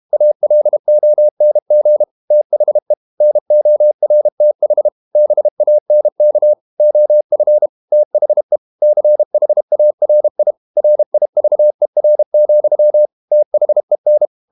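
Morse code, heard as a keyed tone, about 32 wpm, 600 Hz. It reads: ALONG THE NORTH BANK OF THE CHARI RIVER, THEN
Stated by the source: Morse code practice at 32 wpm from CW med Gnister, reading Ken Follett's novel Never in English.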